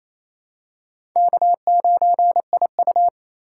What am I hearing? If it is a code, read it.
Morse code, heard as a keyed tone, about 28 wpm, 700 Hz. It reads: K9IU
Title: USA Call Signs - 4 Characters - 8x Character Spacing - 28wpm